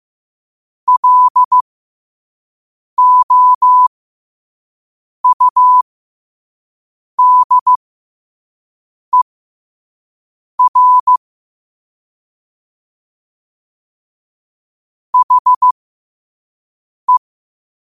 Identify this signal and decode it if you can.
Morse code, heard as a keyed tone, about 15 wpm, spaced as for 5 wpm, 1000 Hz. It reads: LOUDER HE
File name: qotdfiles / 5wpm